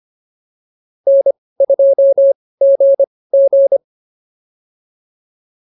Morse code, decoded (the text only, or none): N2GG